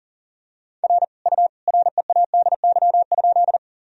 Morse code, heard as a keyed tone, about 40 wpm, 700 Hz. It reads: R U READY?
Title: Sentences from Top 400 Words and Abbreviations 40wpm